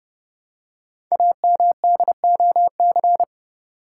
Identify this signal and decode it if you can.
Morse code, heard as a keyed tone, about 30 wpm, 700 Hz. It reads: AMDOC
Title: Code Groups - 5 Letters - 30wpm